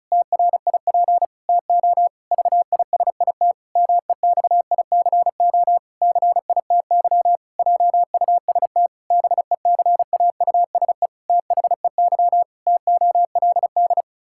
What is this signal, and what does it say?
Morse code, heard as a keyed tone, about 34 wpm, 700 Hz. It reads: TRIP TO VISIT MEXICO CITY JUST BECAUSE THEY TOLD
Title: Pangram Challenge v2 - 35wpm